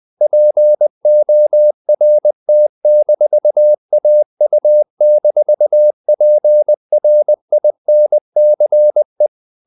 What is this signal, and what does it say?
Morse code, heard as a keyed tone, about 20 wpm, 600 Hz. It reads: PORT-AU-PRINCE
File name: CW / Practice / NationalCapitals20